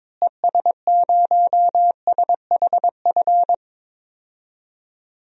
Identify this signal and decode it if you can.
Morse code, heard as a keyed tone, about 22 wpm, 700 Hz. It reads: ES0SHF
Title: DXCC-100 Call Signs - 6 Characters - Repeated 3x - 22wpm